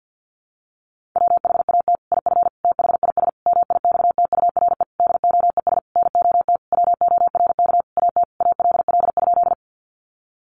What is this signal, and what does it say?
Morse code, heard as a keyed tone, about 50 wpm, 700 Hz. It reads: WHAT IF THIS MIXTURE DOES NOT WORK AT ALL?
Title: Sentences from Top 1500 Words 50wpm